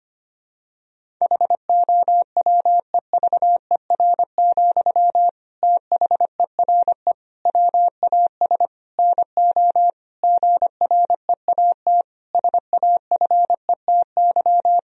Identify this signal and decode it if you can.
Morse code, heard as a keyed tone, about 25 wpm, 700 Hz. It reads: HOWEVER, THERE WAS NO GREAT SAFETY
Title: Sentences from Top 1000 Words - Review Part 4 - 25wpm